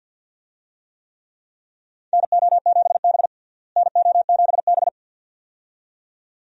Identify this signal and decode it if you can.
Morse code, heard as a keyed tone, about 50 wpm, 700 Hz. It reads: NO7B NO7B
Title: USA Call Signs - 4 Characters - Repeated 2x - 50wpm